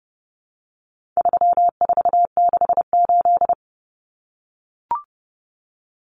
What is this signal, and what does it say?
Morse code, heard as a keyed tone, about 30 wpm, 700 Hz. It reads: 3468
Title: Numbers - 4-Digit 30wpm